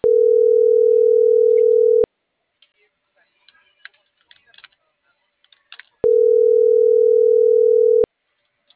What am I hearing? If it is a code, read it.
Ringback tone (North America), 440 + 480 Hz.